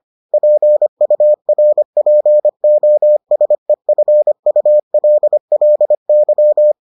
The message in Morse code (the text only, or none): PURPOSEFULLY